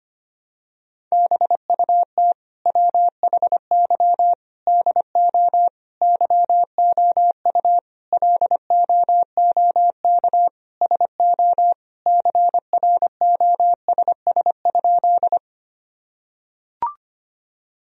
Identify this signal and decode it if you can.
Morse code, heard as a keyed tone, about 25 wpm, 700 Hz. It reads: BUT WHY DO YOU LOOK SO CROSS?
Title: Sentences from Top 1000 Words - Review Part 1 - 25wpm